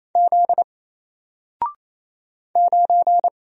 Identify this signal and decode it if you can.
Morse code, heard as a keyed tone, about 27 wpm, 700 Hz. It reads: Z 9
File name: Single-Letter-Number-28wpm